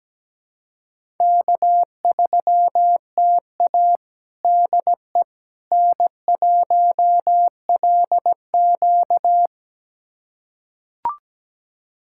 Morse code, heard as a keyed tone, about 17 wpm, 700 Hz. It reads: K3TA DE N1LQ
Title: USA Callsign DE Callsign - 4 Characters - 17wpm